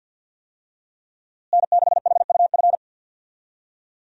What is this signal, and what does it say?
Morse code, heard as a keyed tone, about 52 wpm, 700 Hz. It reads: N6HUF
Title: USA Call Signs - 5 Characters - Repeated 2x - 50wpm